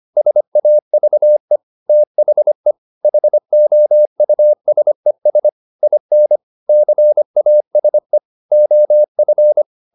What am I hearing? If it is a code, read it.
Morse code, heard as a keyed tone, about 25 wpm, 600 Hz. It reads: SAVE THE HOUSES IN CASE OF